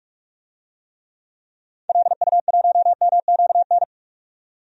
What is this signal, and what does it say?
Morse code, heard as a keyed tone, about 47 wpm, 700 Hz. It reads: LU1MQN
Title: DXCC-100 Call Signs - 6 Characters - 45wpm